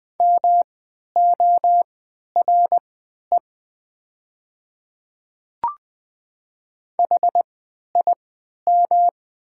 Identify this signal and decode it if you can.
Morse code, heard as a keyed tone, about 20 wpm, spaced as for 11 wpm, 700 Hz. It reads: MORE HIM
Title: 3x Character Spacing - Top 100 Words - 20wpm